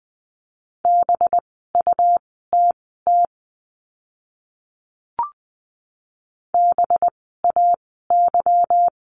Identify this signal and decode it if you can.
Morse code, heard as a keyed tone, about 20 wpm, spaced as for 15 wpm, 700 Hz. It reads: BUTT BAY
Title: 185 Lesson 30f - Letter b - Words 20wpm